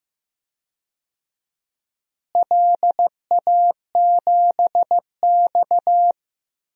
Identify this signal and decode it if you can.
Morse code, heard as a keyed tone, about 15 wpm, 700 Hz. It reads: LA7X
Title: DXCC-100 Call Signs - 4 Characters - Repeated 3x - 15wpm